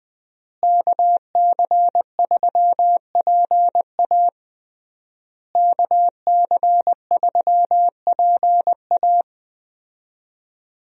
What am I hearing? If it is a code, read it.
Morse code, heard as a keyed tone, about 20 wpm, 700 Hz. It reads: KC3PA KC3PA
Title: USA Call Signs - 5 Characters - Repeated 3x - 20wpm